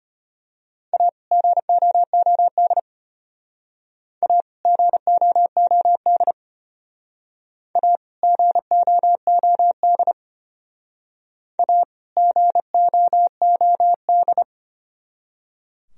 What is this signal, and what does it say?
Morse code, heard as a keyed tone, about 31 wpm, 700 Hz. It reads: A GOOD A GOOD A GOOD A GOOD
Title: Speed-Racing - Sets of 2 Words 25wpm